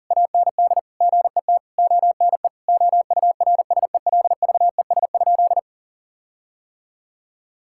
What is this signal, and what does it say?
Morse code, heard as a keyed tone, about 40 wpm, 700 Hz. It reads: AND GET ONE OURSELVES?
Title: Sentences from Top 1600 Words 40wpm